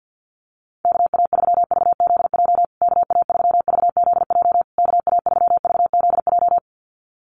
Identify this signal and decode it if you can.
Morse code, heard as a keyed tone, about 50 wpm, 700 Hz. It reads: KA3VZJ KA3VZJ KA3VZJ
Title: USA Call Signs Repeated 3x 50wpm